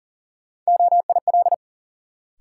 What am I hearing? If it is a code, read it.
Morse code, heard as a keyed tone, about 40 wpm, 700 Hz. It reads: OIL
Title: Sentences from Top 1000 Words - Review Part 4 - New Word Drill - 40wpm